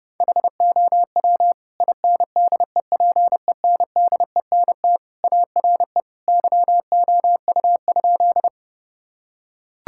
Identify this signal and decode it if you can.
Morse code, heard as a keyed tone, about 32 wpm, 700 Hz. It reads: HOW INDEPENDENT ARE YOU?
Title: Sentences from Top 1000 Words - Review Part 6 - 30wpm